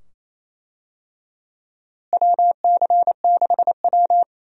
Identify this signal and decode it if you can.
Morse code, heard as a keyed tone, about 28 wpm, 700 Hz. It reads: WC6W
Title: USA Call Signs - 4 Characters - Repeated 2x - 28wpm